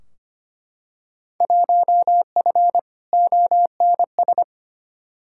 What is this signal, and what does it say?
Morse code, heard as a keyed tone, about 25 wpm, 700 Hz. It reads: 1F ONS
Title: Field Day Exchanges Encore - 25wpm